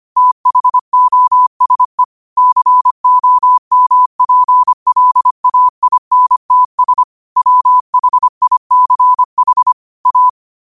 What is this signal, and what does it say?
Morse code, heard as a keyed tone, about 25 wpm, 1000 Hz. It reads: THOSE COMPLAINTS WHICH A
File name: qotdfiles / 25wpm